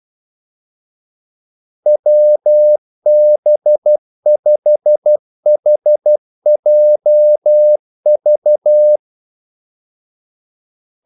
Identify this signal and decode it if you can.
Morse code, heard as a keyed tone, about 12 wpm, 600 Hz. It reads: WB5HJV
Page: 2 ekstra ordmellemrum mellem kaldesignalerne